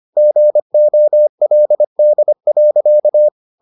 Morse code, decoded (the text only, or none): GOLD.